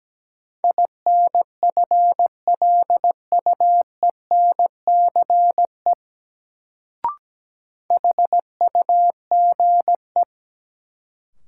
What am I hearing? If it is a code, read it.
Morse code, heard as a keyed tone, about 17 wpm, 700 Hz. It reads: INFLUENCE HUGE
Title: Sentences from Top 1100 Words - New Word Drill - 17wpm